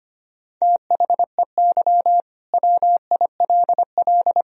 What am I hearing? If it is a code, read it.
Morse code, heard as a keyed tone, about 25 wpm, 700 Hz. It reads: THEY WILL
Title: Sentences from Top 1000 Words - Review Part 2 - 25wpm